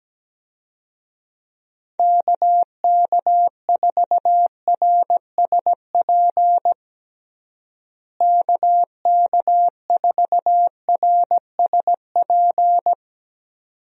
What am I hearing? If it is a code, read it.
Morse code, heard as a keyed tone, about 17 wpm, 700 Hz. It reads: KK4RSP KK4RSP
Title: USA Call Signs - 6 Characters - Repeated 2x - 17wpm